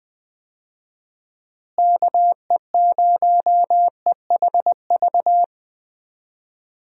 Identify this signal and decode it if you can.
Morse code, heard as a keyed tone, about 20 wpm, 700 Hz. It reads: KE0EHV
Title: USA Call Signs - 6 Characters - Repeated 2x - 20wpm